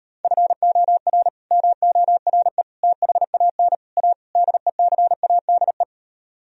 Morse code, heard as a keyed tone, about 38 wpm, 700 Hz. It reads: FOR MORE THAN A DECADE